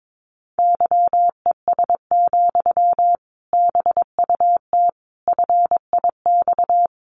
Morse code, heard as a keyed tone, about 22 wpm, 700 Hz. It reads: YES, BUT FIX
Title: Sentences from Top 1400 Words 22wpm